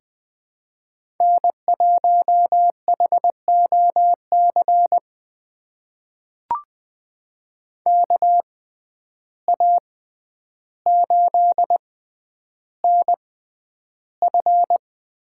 Morse code, heard as a keyed tone, about 20 wpm, spaced as for 10 wpm, 700 Hz. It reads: N1HOC KA8NF
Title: USA Call Signs - 6x Character Spacing - 20wpm